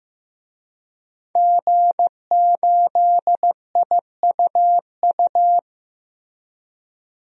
Morse code, heard as a keyed tone, about 15 wpm, 700 Hz. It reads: G8IUU
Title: DXCC-100 Call Signs - 5 Characters - Repeated 2x - 15wpm